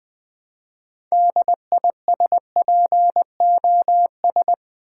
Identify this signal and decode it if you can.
Morse code, heard as a keyed tone, about 20 wpm, 700 Hz. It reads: DISPOS
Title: Build A Word - 3 Letters At A Time - 8-Letter Words - 20wpm